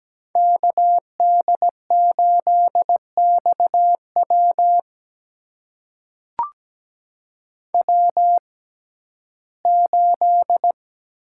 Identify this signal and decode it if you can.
Morse code, heard as a keyed tone, about 17 wpm, spaced as for 12 wpm, 700 Hz. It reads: KD8XW W8
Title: USA Call Signs - 5 Characters - 6x Character Spacing - 17wpm